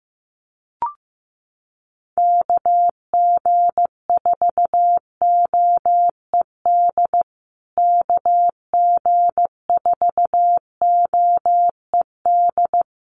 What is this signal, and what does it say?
Morse code, heard as a keyed tone, about 15 wpm, 700 Hz. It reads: KG4OED KG4OED
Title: USA Call Signs Repeated 3x 15wpm